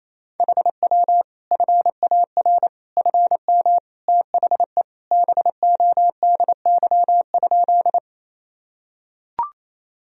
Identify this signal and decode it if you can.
Morse code, heard as a keyed tone, about 28 wpm, 700 Hz. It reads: HW FAR FM THE BODY?
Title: Sentences from Top 400 Words and Abbreviations 28wpm